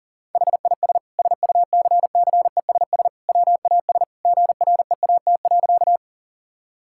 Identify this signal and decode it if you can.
Morse code, heard as a keyed tone, about 37 wpm, 700 Hz. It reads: HIS SUCCESS WAS GREAT.
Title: Sentences from Top 1000 Words - Review Part 6 - 40wpm